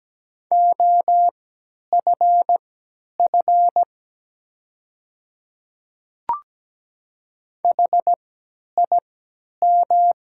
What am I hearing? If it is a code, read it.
Morse code, heard as a keyed tone, about 17 wpm, spaced as for 10 wpm, 700 Hz. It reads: OFF HIM
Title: 3x Character Spacing - Top 200 Words - 17wpm